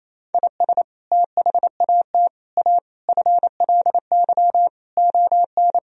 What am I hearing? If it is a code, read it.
Morse code, heard as a keyed tone, about 28 wpm, 700 Hz. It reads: IS THAT A FLY ON
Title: Sentences from Top 1000 Words - Review Part 2 - 28wpm